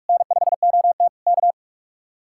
Morse code, heard as a keyed tone, about 49 wpm, 700 Hz. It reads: N5OT K